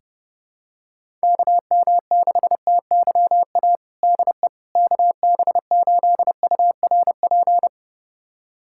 Morse code, heard as a keyed tone, about 30 wpm, 700 Hz. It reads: KM6TYA DE KB8URP
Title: USA Callsign DE Callsign 30wpm